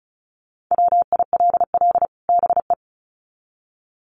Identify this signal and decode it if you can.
Morse code, heard as a keyed tone, about 34 wpm, 700 Hz. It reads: WILL BE
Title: Sets of 2 Words 35wpm